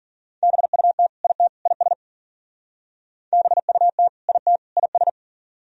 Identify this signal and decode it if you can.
Morse code, heard as a keyed tone, about 55 wpm, 700 Hz. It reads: BUT IT IS BUT IT IS